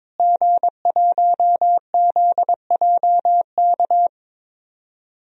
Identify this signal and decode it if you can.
Morse code, heard as a keyed tone, about 22 wpm, 700 Hz. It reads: G1ZJK
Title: DXCC-100 Call Signs - 5 Characters - Repeated 3x - 22wpm